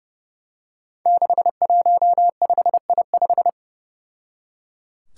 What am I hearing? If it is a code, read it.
Morse code, heard as a keyed tone, about 32 wpm, 700 Hz. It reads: 615I5
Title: Five Challenging Letter-Number Combinations 30wpm